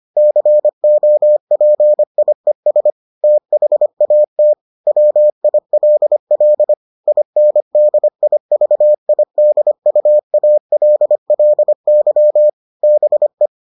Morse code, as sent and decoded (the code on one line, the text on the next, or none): -.-. --- .--. .. ... / - .... .- - / .-- .. .-.. .-.. / .. -. -.. .. ...- .. -.. ..- .- .-.. .-.. -.-- / -... .
COPIES THAT WILL INDIVIDUALLY BE